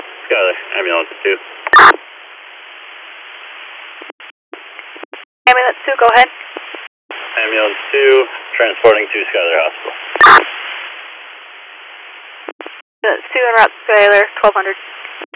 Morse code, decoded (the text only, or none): none